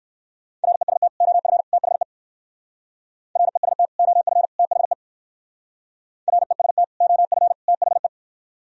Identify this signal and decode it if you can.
Morse code, heard as a keyed tone, about 62 wpm, 700 Hz. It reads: REST OF THE REST OF THE REST OF THE